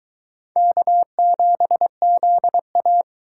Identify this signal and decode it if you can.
Morse code, heard as a keyed tone, about 22 wpm, 700 Hz. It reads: K7ZA